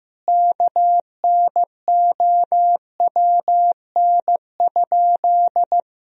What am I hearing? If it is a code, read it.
Morse code, heard as a keyed tone, about 15 wpm, 700 Hz. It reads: KNOWN?
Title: Sentences from Top 1000 Words - Review Part 3 - 15wpm